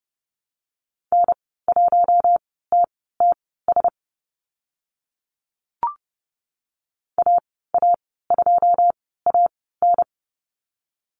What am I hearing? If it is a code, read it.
Morse code, heard as a keyed tone, about 30 wpm, spaced as for 17 wpm, 700 Hz. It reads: N1TTS AA2AN
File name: Lesson 14g - Number 2 - Call Signs 30wpm